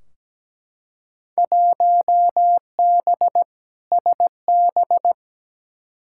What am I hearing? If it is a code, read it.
Morse code, heard as a keyed tone, about 17 wpm, 700 Hz. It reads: 1B SB
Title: Field Day Exchanges - 17wpm